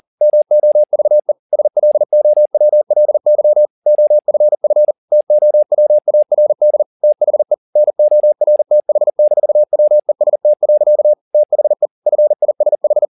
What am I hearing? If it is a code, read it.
Morse code, heard as a keyed tone, about 40 wpm, 600 Hz. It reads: MOVE SLOWLY OFF TOWARD THE NORTH-WEST. THE FISH